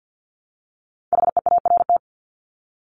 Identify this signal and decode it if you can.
Morse code, heard as a keyed tone, about 50 wpm, 700 Hz. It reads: HEART
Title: Top 1001-1250 Words 50wpm